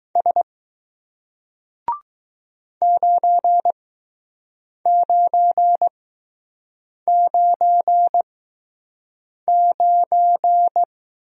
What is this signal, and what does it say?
Morse code, heard as a keyed tone, about 20 wpm, spaced as for 6 wpm, 700 Hz. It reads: S 9999